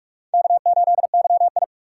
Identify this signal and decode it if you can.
Morse code, heard as a keyed tone, about 42 wpm, 700 Hz. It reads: K8YI
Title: USA Call Signs - 4 Characters - 6x Character Spacing - 45wpm